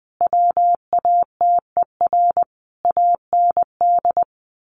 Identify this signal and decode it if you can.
Morse code, heard as a keyed tone, about 20 wpm, 700 Hz. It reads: WATER AND
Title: Sentences from Top 1300 Words 20wpm